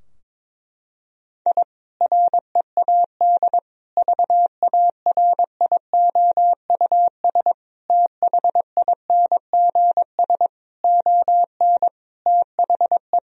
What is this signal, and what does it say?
Morse code, heard as a keyed tone, about 22 wpm, 700 Hz. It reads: I READ VARIOUS THINGS ON THE
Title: Sentences from Top 1000 Words - Review Part 4 - 22wpm